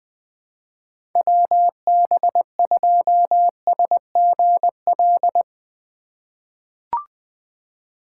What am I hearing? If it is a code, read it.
Morse code, heard as a keyed tone, about 20 wpm, 700 Hz. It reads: WB2SGL
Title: USA Call Signs - 6 Characters - 2x Character Spacing - 20wpm